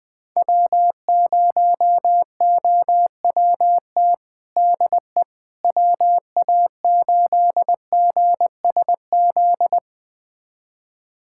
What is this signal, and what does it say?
Morse code, heard as a keyed tone, about 20 wpm, 700 Hz. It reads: W0OWT DE WA8GSZ
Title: USA Callsign DE Callsign 20wpm